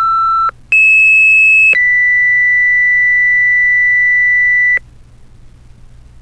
Two-tone page, heard as a one-second tone, then a three-second tone, about 2600 then 2000 Hz, with noise. It is heard from 0.7 to 4.8 s.
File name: Fire-Pager